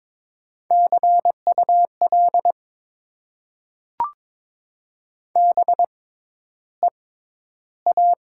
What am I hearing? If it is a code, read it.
Morse code, heard as a keyed tone, about 22 wpm, spaced as for 11 wpm, 700 Hz. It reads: CUL BEA
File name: Top 90 QSO Elements - 6x Character Spacing - 22wpm